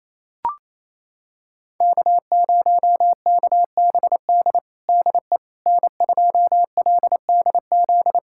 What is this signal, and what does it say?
Morse code, heard as a keyed tone, about 28 wpm, 700 Hz. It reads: K0KBD DE N2LDZ
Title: USA Callsign DE Callsign - 5 Characters - 28wpm